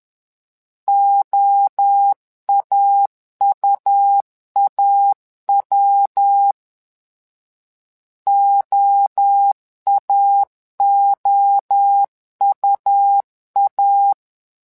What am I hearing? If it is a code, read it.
Morse code, heard as a keyed tone, about 10 wpm, 800 Hz. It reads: OAUAW OAOUA
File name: LEZIONI / 3 MWRO / 3 MWRO 10wpm 2148